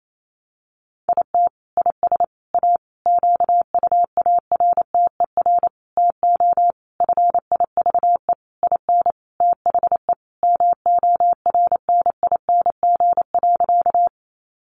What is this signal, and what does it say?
Morse code, heard as a keyed tone, about 28 wpm, 700 Hz. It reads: IT IS A QUARTER TO FIVE IN THE MORNING.
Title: Sentences from Top 1300 Words 28wpm